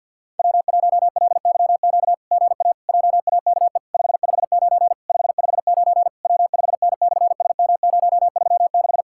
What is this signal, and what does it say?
Morse code, heard as a keyed tone, about 50 wpm, 700 Hz. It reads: W1LYQ GA JAKE 559 559 W5N/SM036
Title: SOTA QSO Practice 50wpm